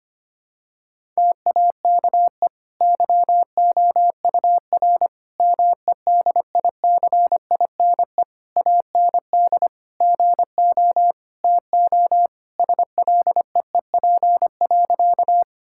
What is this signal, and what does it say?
Morse code, heard as a keyed tone, about 25 wpm, 700 Hz. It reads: TAKE YOUR MEDICINE AND GO TO SLEEP.
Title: Sentences from Top 1000 Words - Review Part 6 - 25wpm